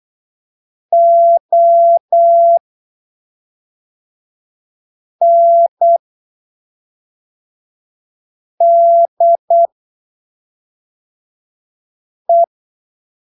Morse code, heard as a keyed tone, about 8 wpm, spaced as for 3 wpm, 680 Hz. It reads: ONDE
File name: arquivos / texto / texto 13 08ppm